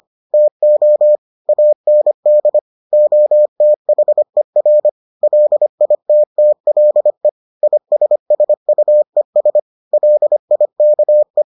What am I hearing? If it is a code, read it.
Morse code, heard as a keyed tone, about 25 wpm, 600 Hz. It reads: TO AND OTHER LITTLE ISSUES LIKE